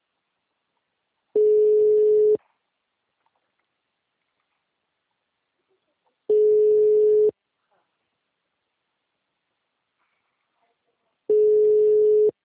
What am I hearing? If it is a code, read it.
Ringback tone (Europe, 425 Hz), with speech.
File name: out- -2006- - - .69920